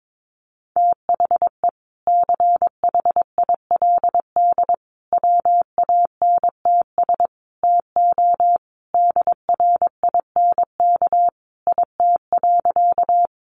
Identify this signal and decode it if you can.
Morse code, heard as a keyed tone, about 22 wpm, 700 Hz. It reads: THE CHILD WANTS TO DRINK IT.
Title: Additional Easy Phrases 22wpm